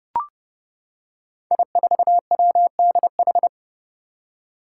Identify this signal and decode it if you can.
Morse code, heard as a keyed tone, about 30 wpm, 700 Hz. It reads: I4WDH
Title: Five Challenging Letter-Number Combinations 30wpm